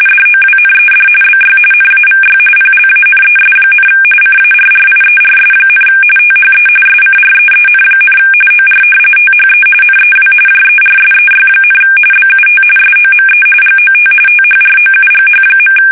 RTTY, about 45.45 baud. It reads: DOWNED BY NAVY PLANES, CARRIERS, THE SCREEN, OR THE NEW JERSEY, BUT SOME GOT THROUGH.
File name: fsk850 / battleshipnj